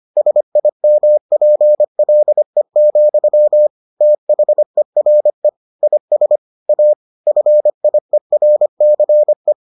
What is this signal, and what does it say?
Morse code, heard as a keyed tone, about 25 wpm, 600 Hz. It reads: SIMPLE, THERE IS A FIERCE